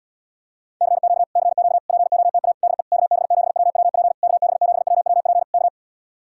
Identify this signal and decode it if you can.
Morse code, heard as a keyed tone, about 77 wpm, 700 Hz. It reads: CQ CQ POTA DE KB7WWP KB7WWP K